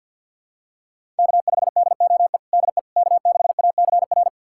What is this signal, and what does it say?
Morse code, heard as a keyed tone, about 51 wpm, 700 Hz. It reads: K5DOE DE K6ACR
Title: USA Callsign DE Callsign - 5 Characters - 50wpm